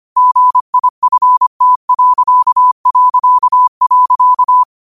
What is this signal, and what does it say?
Morse code, heard as a keyed tone, about 25 wpm, 1000 Hz. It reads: GIFT...